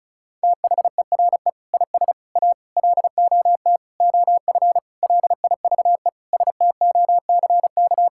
Morse code, heard as a keyed tone, about 36 wpm, 700 Hz. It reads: THERE IS A LOT OF LIVE STOCK
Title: Sentences from Top 1000 Words - Review Part 5 - 35wpm